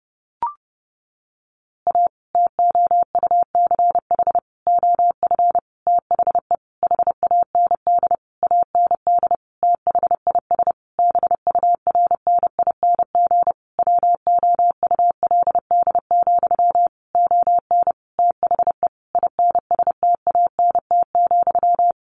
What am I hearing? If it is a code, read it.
Morse code, heard as a keyed tone, about 30 wpm, 700 Hz. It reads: A TOUCH OF THE HAND AND THIS BURNING WOULD, ON THE INSTANT,
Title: A Sound Of Thunder 30wpm